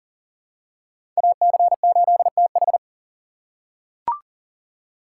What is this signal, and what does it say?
Morse code, heard as a keyed tone, about 40 wpm, 700 Hz. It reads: AC8TH
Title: USA Call Signs - 5 Characters - 3x Character Spacing - 40wpm